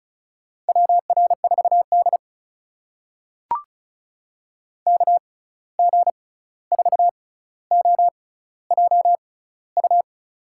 Morse code, heard as a keyed tone, about 36 wpm, spaced as for 16 wpm, 700 Hz. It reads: WR4D KG4OJU
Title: USA Call Signs - 6x Character Spacing - 35wpm